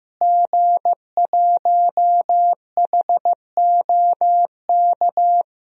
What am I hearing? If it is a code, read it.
Morse code, heard as a keyed tone, about 15 wpm, 700 Hz. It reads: G1HOK
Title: DXCC-100 Call Signs - 5 Characters - Repeated 3x - 15wpm